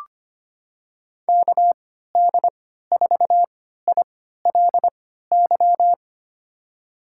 Morse code, heard as a keyed tone, about 25 wpm, spaced as for 14 wpm, 700 Hz. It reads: KD4ILY
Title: USA Call Signs - 6 Characters - 3x Character Spacing - 25wpm